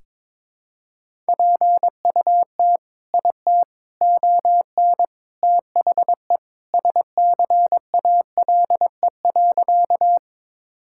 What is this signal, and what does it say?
Morse code, heard as a keyed tone, about 22 wpm, 700 Hz. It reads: PUT IT ON THE SCALE.
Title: Sentences from Top 1000 Words - Review Part 5 - 22wpm